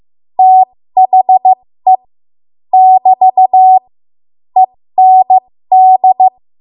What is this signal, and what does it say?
Morse code, heard as a keyed tone, about 15 wpm, spaced as for 13 wpm, 750 Hz. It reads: THE <BT> END
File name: Morse / 13WPM